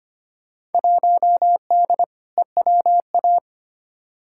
Morse code, heard as a keyed tone, about 25 wpm, 700 Hz. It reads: 1D EWA
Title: Field Day Exchanges - 25wpm